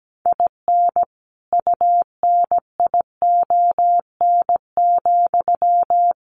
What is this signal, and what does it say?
Morse code, heard as a keyed tone, about 17 wpm, 700 Hz. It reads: IN UNION,
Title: Sentences from Top 1400 Words 17wpm